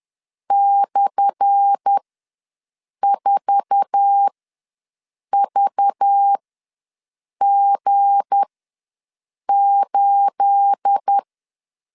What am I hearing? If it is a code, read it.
Morse code, heard as a keyed tone, about 10 wpm, spaced as for 6 wpm, 800 Hz. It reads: /4VG8
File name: LEZIONI / 10 SIGNS / 10A SLASH ws 10wpm 4121